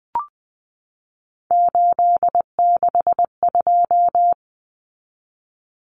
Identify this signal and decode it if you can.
Morse code, heard as a keyed tone, about 20 wpm, 700 Hz. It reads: 862
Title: Numbers - 3-Digit 20wpm